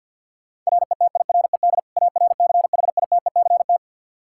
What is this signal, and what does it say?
Morse code, heard as a keyed tone, about 53 wpm, 700 Hz. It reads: RETIRED ARCHITECT
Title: Practice QSO - Exchange 3 - More Info - 50wpm